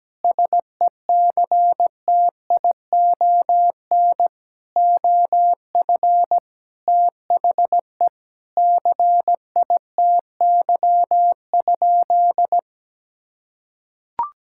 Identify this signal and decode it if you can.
Morse code, heard as a keyed tone, about 17 wpm, 700 Hz. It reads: SECTION OF THE CITY?
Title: Sentences from Top 1000 Words - Review Part 5 - 17wpm